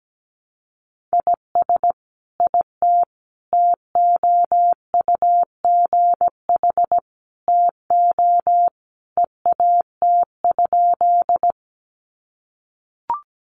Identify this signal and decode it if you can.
Morse code, heard as a keyed tone, about 17 wpm, 700 Hz. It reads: IS IT TOUGH TO EAT?
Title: Sentences from Top 1200 Words 17wpm